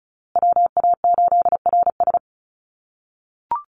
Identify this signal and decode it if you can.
Morse code, heard as a keyed tone, about 34 wpm, 700 Hz. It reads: WA8RS
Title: USA Call Signs Repeated 3x 35wpm